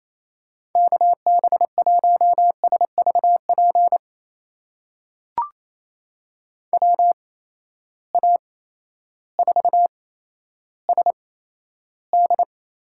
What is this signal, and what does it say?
Morse code, heard as a keyed tone, about 27 wpm, spaced as for 13 wpm, 700 Hz. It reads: KB1SVP WA4SD
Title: USA Call Signs - 6 Characters - 8x Character Spacing - 28wpm